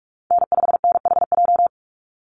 Morse code, heard as a keyed tone, about 44 wpm, 700 Hz. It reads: N5NHJ
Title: Call Signs - 45wpm